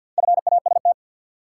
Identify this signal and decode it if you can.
Morse code, heard as a keyed tone, about 53 wpm, 700 Hz. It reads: FAST
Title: Four Letter Words - 50wpm